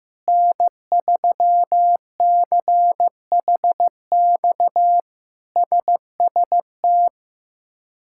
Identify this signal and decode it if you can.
Morse code, heard as a keyed tone, about 15 wpm, 700 Hz. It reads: N3CHX SST
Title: K1USN SST QSO Practice - 15wpm